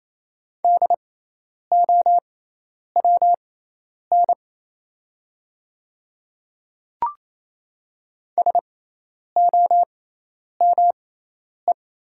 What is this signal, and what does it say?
Morse code, heard as a keyed tone, about 27 wpm, spaced as for 10 wpm, 700 Hz. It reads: DOWN SOME
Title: 6x Character Spacing - Top 100 Words - 28wpm